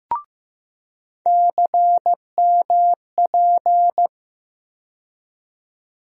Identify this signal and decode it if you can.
Morse code, heard as a keyed tone, about 15 wpm, 700 Hz. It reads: CMP
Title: USA Call Sign Postfixes - 3 Letters 15wpm